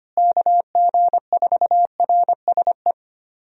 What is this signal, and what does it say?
Morse code, heard as a keyed tone, about 25 wpm, 700 Hz. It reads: KG4RSE
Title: USA Call Signs - 6 Characters - Repeated 2x - 25wpm